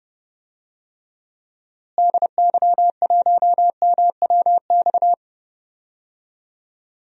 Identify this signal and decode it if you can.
Morse code, heard as a keyed tone, about 29 wpm, 700 Hz. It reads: DY1MWX
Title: DXCC-100 Call Signs - 6 Characters - Repeated 3x - 30wpm